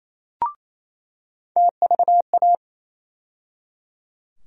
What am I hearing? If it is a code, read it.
Morse code, heard as a keyed tone, about 27 wpm, 700 Hz. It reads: TVA
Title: USA Call Sign Postfixes - 3 Letters 28wpm